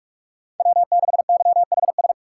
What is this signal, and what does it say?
Morse code, heard as a keyed tone, about 45 wpm, 700 Hz. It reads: W6YHS
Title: USA Call Signs - 5 Characters - 4x Character Spacing - 45wpm